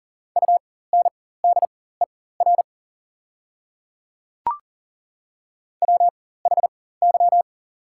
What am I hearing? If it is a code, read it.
Morse code, heard as a keyed tone, about 40 wpm, spaced as for 19 wpm, 690 Hz. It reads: UNDER WHY
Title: 4x Character Spacing - Top 200 Words - 40wpm